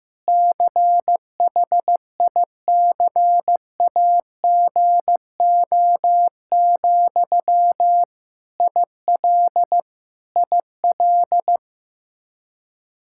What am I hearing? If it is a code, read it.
Morse code, heard as a keyed tone, about 15 wpm, 700 Hz. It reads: CHICAGO, IL IL
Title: K1USN SST QSO Practice 15wpm